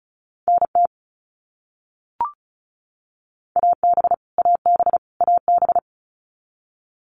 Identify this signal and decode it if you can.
Morse code, heard as a keyed tone, about 34 wpm, 700 Hz. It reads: NT AB AB AB
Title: Canadian Provinces and Territories - Repeated 3x - 35wpm